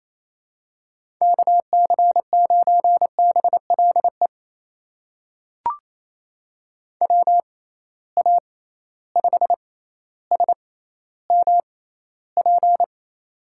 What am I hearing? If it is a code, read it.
Morse code, heard as a keyed tone, about 28 wpm, spaced as for 14 wpm, 700 Hz. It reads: KC9BLE WA5SMP
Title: USA Call Signs - 6 Characters - 6x Character Spacing - 28wpm